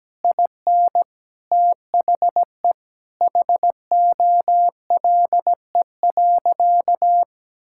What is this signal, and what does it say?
Morse code, heard as a keyed tone, about 17 wpm, 700 Hz. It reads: IN THE HOLE.
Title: Sentences from Top 1600 Words 17wpm